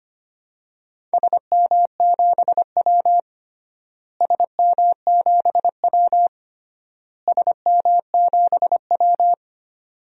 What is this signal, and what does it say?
Morse code, heard as a keyed tone, about 25 wpm, 700 Hz. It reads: SM7W SM7W SM7W
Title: DXCC-100 Call Signs - 4 Characters - Repeated 3x - 25wpm